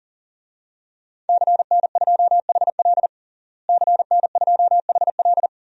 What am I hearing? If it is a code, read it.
Morse code, heard as a keyed tone, about 40 wpm, 700 Hz. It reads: CN2HL CN2HL
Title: DXCC-100 Call Signs - 5 Characters - Repeated 3x - 40wpm